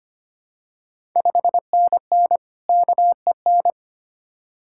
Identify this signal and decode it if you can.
Morse code, heard as a keyed tone, about 25 wpm, 690 Hz. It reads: 5NN KEN
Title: Ontario QSO Party Exchange 25wpm